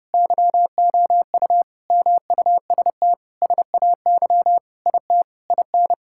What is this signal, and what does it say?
Morse code, heard as a keyed tone, about 29 wpm, 700 Hz. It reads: YOU MUST SAY IT IN
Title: Sentences from Top 1600 Words 30wpm